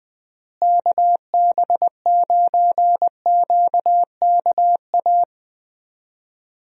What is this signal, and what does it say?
Morse code, heard as a keyed tone, about 20 wpm, 700 Hz. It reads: KB9QKA
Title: USA Call Signs - 6 Characters - Repeated 2x - 20wpm